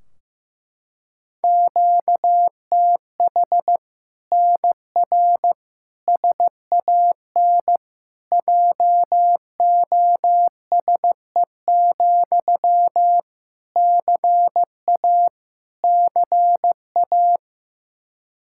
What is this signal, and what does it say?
Morse code, heard as a keyed tone, about 15 wpm, 700 Hz. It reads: QTH NR SAN JOSE, CA CA
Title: Practice QSO - Exchange 1 - Beginning - 15wpm